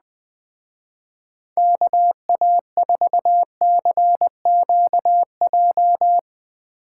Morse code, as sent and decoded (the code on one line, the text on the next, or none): -.- .- ....- -.-. --.- .---
KA4CQJ